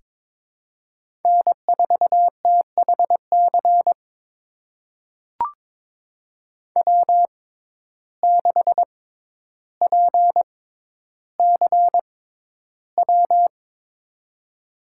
Morse code, heard as a keyed tone, about 22 wpm, spaced as for 11 wpm, 700 Hz. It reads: N4THC W6PCW
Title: USA Call Signs - 5 Characters - 6x Character Spacing - 22wpm